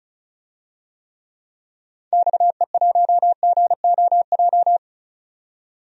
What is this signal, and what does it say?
Morse code, heard as a keyed tone, about 36 wpm, 700 Hz. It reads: XE1GOJ
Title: DXCC-100 Call Signs - 6 Characters - 35wpm